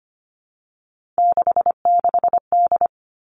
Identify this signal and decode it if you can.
Morse code, heard as a keyed tone, about 25 wpm, 700 Hz. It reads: 66D
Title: Three Challenging Letter-Number Combinations 25wpm